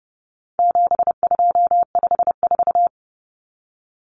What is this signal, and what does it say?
Morse code, heard as a keyed tone, about 30 wpm, 700 Hz. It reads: 7254
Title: Numbers - 4-Digit 30wpm